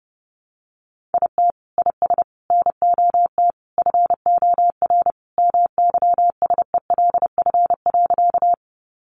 Morse code, heard as a keyed tone, about 30 wpm, 700 Hz. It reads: IT IS NOT FOR MYSELF.
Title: Sentences from Top 1300 Words 30wpm